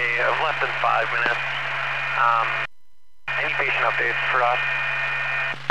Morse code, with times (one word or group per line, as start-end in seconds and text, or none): none